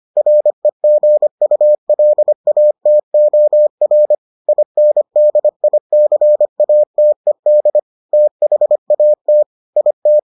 Morse code, heard as a keyed tone, about 25 wpm, 600 Hz. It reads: REGULATOR INDICATED THAT IT